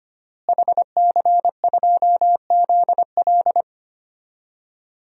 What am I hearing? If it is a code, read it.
Morse code, heard as a keyed tone, about 25 wpm, 700 Hz. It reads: HC2ZL
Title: DXCC-100 Call Signs - 5 Characters - Repeated 2x - 25wpm